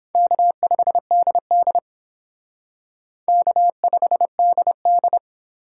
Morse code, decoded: K5DD K5DD

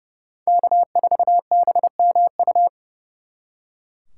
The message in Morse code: K4BMU

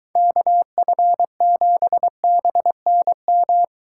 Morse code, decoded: KF7BNM

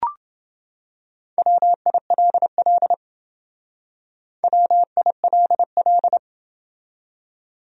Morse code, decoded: WILL WILL

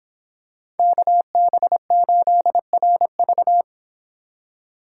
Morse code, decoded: KB8RV

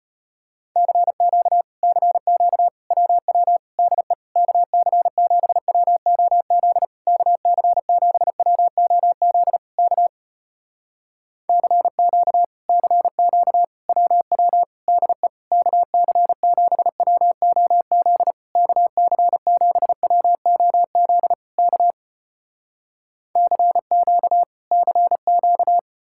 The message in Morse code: CQ CQ WW DE KC7WOZ KC7WOZ K CQ CQ WW DE KC7WOZ KC7WOZ K CQ CQ

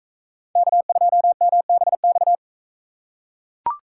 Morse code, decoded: K2MBX

